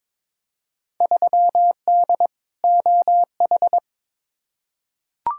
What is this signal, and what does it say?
Morse code, heard as a keyed tone, about 22 wpm, 700 Hz. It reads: 3D OH
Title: Field Day Exchanges - 22wpm